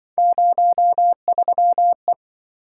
Morse code, heard as a keyed tone, about 24 wpm, 700 Hz. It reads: 03E
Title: Speed-Racing - Three Letter-Number Combinations - 20wpm